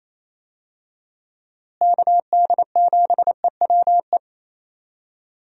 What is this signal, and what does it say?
Morse code, heard as a keyed tone, about 28 wpm, 700 Hz. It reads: KD7EWE
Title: USA Call Signs - 6 Characters - Repeated 3x - 28wpm